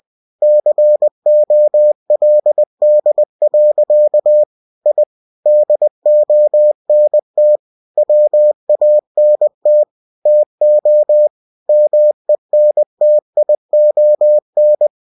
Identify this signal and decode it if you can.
Morse code, heard as a keyed tone, about 20 wpm, 600 Hz. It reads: COLD. I DONT WANT TO MENTION
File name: cw-books / winnie-the-pooh / winnie-20wpm-page-0087